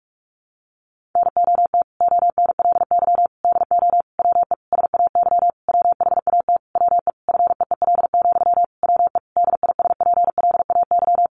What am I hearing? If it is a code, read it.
Morse code, heard as a keyed tone, about 44 wpm, 700 Hz. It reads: NOT ONLY DO WE SAY WHAT WE FEEL, WE DISPLAY
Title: Sentences from Top 1200 Words 45wpm